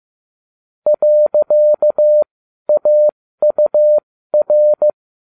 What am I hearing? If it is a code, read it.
Morse code, heard as a keyed tone, about 15 wpm, spaced as for 12 wpm, 600 Hz. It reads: .AUR